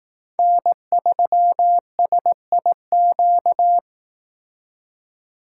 Morse code, heard as a keyed tone, about 18 wpm, 700 Hz. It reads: N3SIQ
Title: Speed Racing - USA 1x3 Call Signs 15wpm